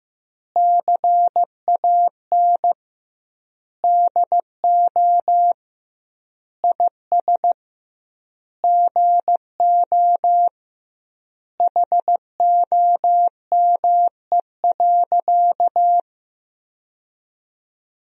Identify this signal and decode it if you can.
Morse code, heard as a keyed tone, about 15 wpm, 700 Hz. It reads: CAN DO IS GO HOME.
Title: 2x Word-Spacing - Sentences from Top 200 Words - 15wpm